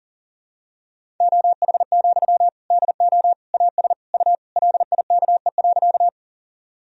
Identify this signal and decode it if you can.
Morse code, heard as a keyed tone, about 40 wpm, 700 Hz. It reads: OH, DO AS U LIKE.
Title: Sentences from Top 500 Words and Abbreviations 40wpm